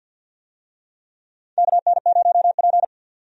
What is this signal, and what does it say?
Morse code, heard as a keyed tone, about 53 wpm, 700 Hz. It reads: KN0P
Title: USA Call Signs - 4 Characters - 50wpm